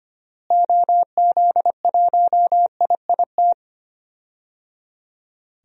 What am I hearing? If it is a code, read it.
Morse code, heard as a keyed tone, about 25 wpm, 700 Hz. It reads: OZ1IIT
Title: DXCC-100 Call Signs - 6 Characters - Repeated 3x - 25wpm